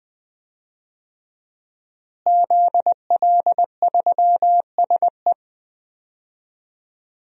Morse code, heard as a keyed tone, about 20 wpm, 700 Hz. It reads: ZL3SE
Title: DXCC-100 Call Signs - 5 Characters - Repeated 2x - 20wpm